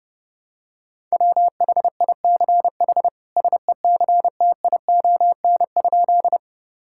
Morse code, heard as a keyed tone, about 32 wpm, 700 Hz. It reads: WHICH SECTION?
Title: Sentences from Top 1000 Words - Review Part 5 - 30wpm